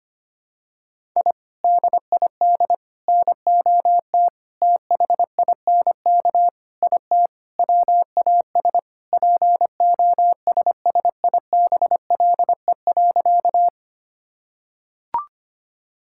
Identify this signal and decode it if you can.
Morse code, heard as a keyed tone, about 25 wpm, 700 Hz. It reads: I DID NOT THINK IT WAS POSSIBLE.
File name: practice / 25wpm / Sentences with 4-Word Phrases - 25wpm